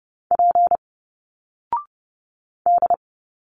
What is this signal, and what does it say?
Morse code, heard as a keyed tone, about 30 wpm, 700 Hz. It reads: P D
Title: Single Letters 30wpm